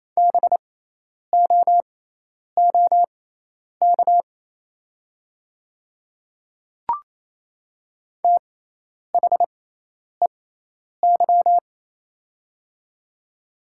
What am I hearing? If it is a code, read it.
Morse code, heard as a keyed tone, about 28 wpm, spaced as for 10 wpm, 690 Hz. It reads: BOOK THEY
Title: 6x Character Spacing - Top 300 Words - 28wpm